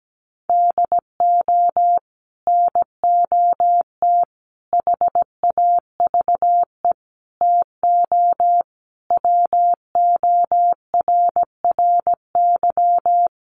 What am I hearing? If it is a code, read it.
Morse code, heard as a keyed tone, about 17 wpm, 700 Hz. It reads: DO NOT HAVE TO WORRY
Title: Sets of 5 Words 17wpm